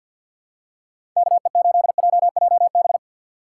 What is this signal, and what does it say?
Morse code, heard as a keyed tone, about 54 wpm, 700 Hz. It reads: K E 8 J J B
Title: USA Call Signs - 6 Characters - 8x Character Spacing - 50wpm